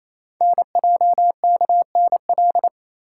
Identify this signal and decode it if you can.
Morse code, heard as a keyed tone, about 29 wpm, 700 Hz. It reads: NJKNL
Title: Code Groups - 5 Letter:Numbers - 28wpm